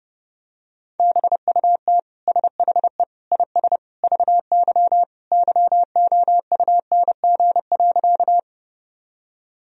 Morse code, heard as a keyed tone, about 32 wpm, 700 Hz. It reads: BUT SHE IS VY YOUNG.
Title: Sentences from Top 400 Words and Abbreviations 30wpm